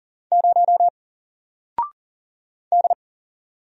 Morse code, heard as a keyed tone, about 40 wpm, 700 Hz. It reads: 0 D
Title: Single Letter-Number Alt 40wpm